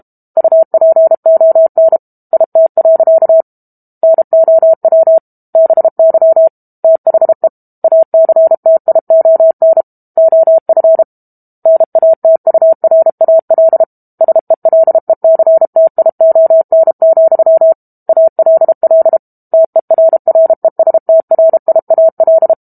Morse code, heard as a keyed tone, about 32 wpm, 650 Hz. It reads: UPON IT. NOW BY THE ACTION OF NATURAL SELECTION, ALL TERRESTRIAL
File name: Book Two / 06-Chapter 6 - 32 WPM